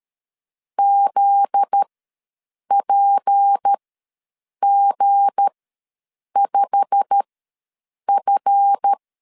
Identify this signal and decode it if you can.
Morse code, heard as a keyed tone, about 13 wpm, spaced as for 7 wpm, 800 Hz. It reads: ZPG5F
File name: LEZIONI / 8 YJ45 / 8 YJ45 ws 12wpm 3146